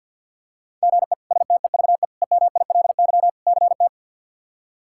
Morse code, heard as a keyed tone, about 46 wpm, 700 Hz. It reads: GE STEVE EMILY CT
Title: K1USN SST QSO Practice - 50wpm